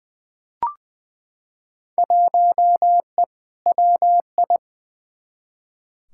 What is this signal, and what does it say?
Morse code, heard as a keyed tone, about 21 wpm, 700 Hz. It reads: 1E WI
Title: Field Day Exchanges - 20wpm